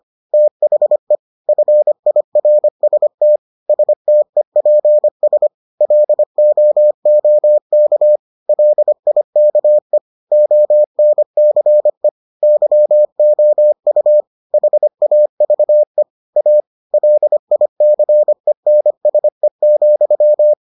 Morse code, heard as a keyed tone, about 25 wpm, 600 Hz. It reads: THE FIRST STEPS LOOK LIKE ONCE YOU HAVE A LICENSE,